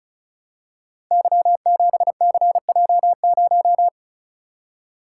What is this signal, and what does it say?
Morse code, heard as a keyed tone, about 36 wpm, 700 Hz. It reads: Y7CJ0
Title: Code Groups - 5 Letter:Numbers - 35wpm